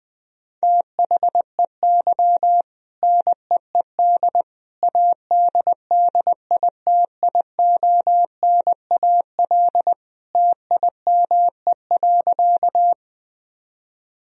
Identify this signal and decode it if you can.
Morse code, heard as a keyed tone, about 20 wpm, 700 Hz. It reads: THEY NEED ADDITIONAL TIME.
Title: Sentences from Top 1000 Words - Review Part 4 - 20wpm